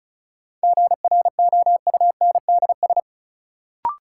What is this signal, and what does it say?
Morse code, heard as a keyed tone, about 34 wpm, 700 Hz. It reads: GROUNDS
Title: Sentences from Top 1300 Words - New Word Drill - 35wpm